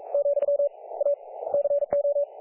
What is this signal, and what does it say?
Morse code, heard as a keyed tone, about 44 wpm, 570 Hz. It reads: , A CO
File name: dlazaawansowanych / nagraniazpasma / websdr recording 3564.7kHz